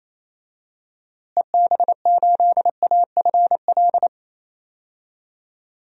Morse code, heard as a keyed tone, about 28 wpm, 700 Hz. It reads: EB8AFL
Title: DXCC-100 Call Signs - 6 Characters - Repeated 2x - 28wpm